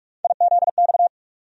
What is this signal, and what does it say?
Morse code, heard as a keyed tone, about 43 wpm, 700 Hz. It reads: IZX
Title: USA Call Sign Postfixes - 3 Letters 45wpm